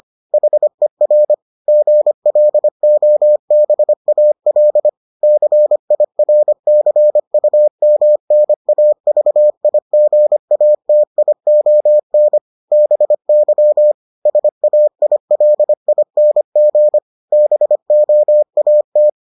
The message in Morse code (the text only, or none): HER GLOBAL CIRCUMNAVIGATION BY SAILING BOAT